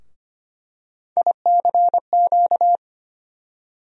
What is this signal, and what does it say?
Morse code, heard as a keyed tone, about 25 wpm, 700 Hz. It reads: ICQ
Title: USA Call Sign Postfixes - 3 Letters 25wpm